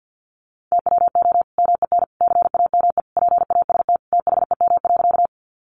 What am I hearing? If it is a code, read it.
Morse code, heard as a keyed tone, about 50 wpm, 700 Hz. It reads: TWO MEN CAME PAST THEM.
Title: Sentences from Top 600 Words 50wpm